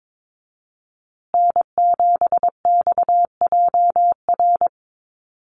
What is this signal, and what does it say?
Morse code, heard as a keyed tone, about 22 wpm, 700 Hz. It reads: N7XJR
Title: USA Call Signs Repeated 3x 22wpm